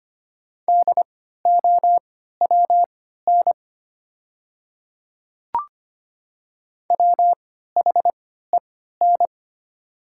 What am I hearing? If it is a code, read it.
Morse code, heard as a keyed tone, about 25 wpm, spaced as for 14 wpm, 690 Hz. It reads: DOWN WHEN
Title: 3x Character Spacing - Top 200 Words - 25wpm